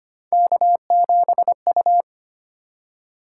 Morse code, heard as a keyed tone, about 25 wpm, 700 Hz. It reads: K7U
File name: Three Letter-Number Combinations - 25wpm